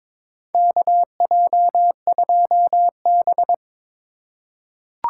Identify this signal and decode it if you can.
Morse code, heard as a keyed tone, about 22 wpm, 700 Hz. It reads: KJ2B
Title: USA Call Signs - 4 Characters - 3x Character Spacing - 22wpm